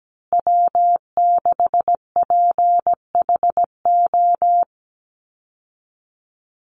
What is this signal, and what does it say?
Morse code, heard as a keyed tone, about 17 wpm, 700 Hz. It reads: W6PHO
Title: Call Signs Repeated 2x - 17wpm